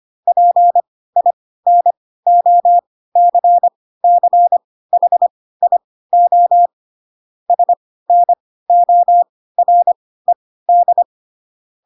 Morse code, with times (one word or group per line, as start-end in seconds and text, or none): 0.3-6.7 PINOCCHIO
7.5-11.0 SNORED